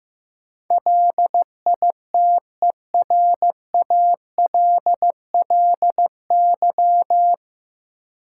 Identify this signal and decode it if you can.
Morse code, heard as a keyed tone, about 15 wpm, 700 Hz. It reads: LITERALLY
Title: Sentences from Top 1400 Words - New Word Drill - 15wpm